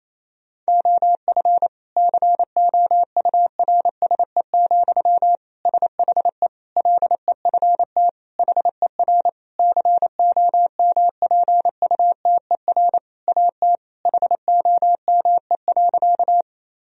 Morse code, heard as a keyed tone, about 28 wpm, 700 Hz. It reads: OF COURSE, SHE LEFT HER COMPUTER AT HOME.
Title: Sentences from Top 1000 Words - Review Part 3 - 28wpm